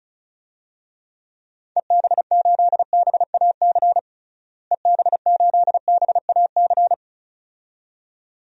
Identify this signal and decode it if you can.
Morse code, heard as a keyed tone, about 34 wpm, 700 Hz. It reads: EB8BAC EB8BAC
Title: DXCC-100 Call Signs - 6 Characters - Repeated 2x - 35wpm